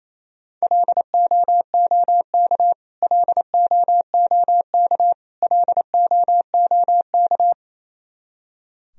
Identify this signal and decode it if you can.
Morse code, heard as a keyed tone, about 27 wpm, 700 Hz. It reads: LOOK LOOK LOOK
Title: Top 100 Words Repeated 28wpm